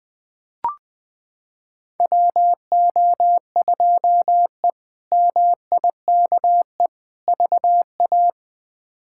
Morse code, heard as a keyed tone, about 20 wpm, 700 Hz. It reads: WO2E MIKE VA
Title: K1USN SST QSO Practice - 20wpm